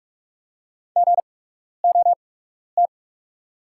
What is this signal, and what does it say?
Morse code, heard as a keyed tone, about 45 wpm, spaced as for 12 wpm, 690 Hz. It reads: GOT
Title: 8x Character Spacing - Top 300 Words - 45wpm